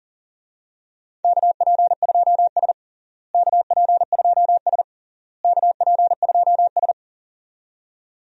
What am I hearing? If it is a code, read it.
Morse code, heard as a keyed tone, about 40 wpm, 700 Hz. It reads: KP2S KP2S KP2S
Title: DXCC-100 Call Signs - 4 Characters - Repeated 3x - 40wpm